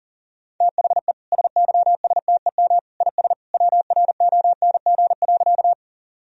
Morse code, heard as a keyed tone, about 37 wpm, 700 Hz. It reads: THE SYSTEM IS WRONG.
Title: Sentences from Top 1000 Words - Review Part 3 - 40wpm